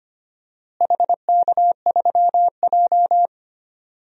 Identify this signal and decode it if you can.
Morse code, heard as a keyed tone, about 25 wpm, 700 Hz. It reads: HK3J
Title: DXCC-100 Call Signs - 4 Characters - Repeated 2x - 25wpm